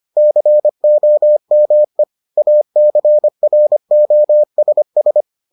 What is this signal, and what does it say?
Morse code, heard as a keyed tone, about 25 wpm, 600 Hz. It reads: COME ACROSS